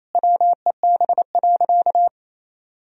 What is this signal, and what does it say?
Morse code, heard as a keyed tone, about 27 wpm, 700 Hz. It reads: WEB.
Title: Sentences from Top 1000 Words - Review Part 4 - 28wpm